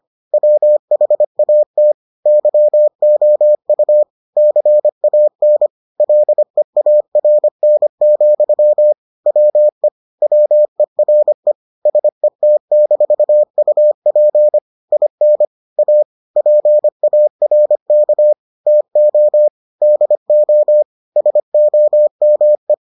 Morse code, heard as a keyed tone, about 25 wpm, 600 Hz. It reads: WHAT YOU CAN LEARN, WE WERE SET-UP IN A PARK TO DO SOME